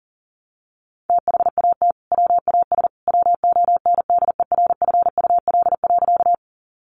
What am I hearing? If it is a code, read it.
Morse code, heard as a keyed tone, about 40 wpm, 700 Hz. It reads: THAT WAS WONDERFUL.
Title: Sentences from Top 1100 Words 40wpm